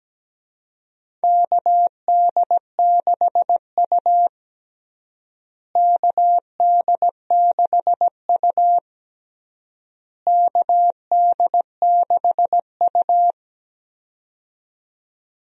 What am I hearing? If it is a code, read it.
Morse code, heard as a keyed tone, about 17 wpm, 700 Hz. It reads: KD6U KD6U KD6U
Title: USA Call Signs - 4 Characters - Repeated 3x - 17wpm